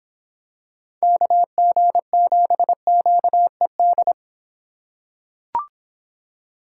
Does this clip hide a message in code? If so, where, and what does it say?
Yes: Morse code from 1.0 to 4.1 s, heard as a keyed tone, about 26 wpm, 700 Hz. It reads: KG7QED